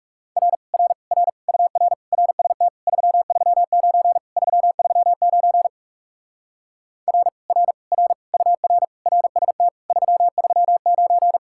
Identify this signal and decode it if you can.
Morse code, heard as a keyed tone, about 40 wpm, 700 Hz. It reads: R R R UR RST 339 339 R R R UR RST 339